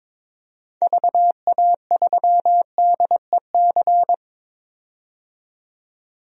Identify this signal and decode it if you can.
Morse code, heard as a keyed tone, about 22 wpm, 700 Hz. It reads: VA3DEC
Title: DXCC-100 Call Signs - 6 Characters - 22wpm